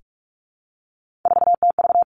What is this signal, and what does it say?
Morse code, heard as a keyed tone, about 46 wpm, 700 Hz. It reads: 4TV
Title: Three Challenging Letter-Number Combinations 45wpm